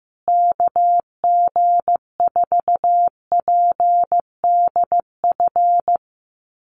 Morse code, heard as a keyed tone, about 15 wpm, 700 Hz. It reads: KG4PDF
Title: USA Call Signs Repeated 3x 15wpm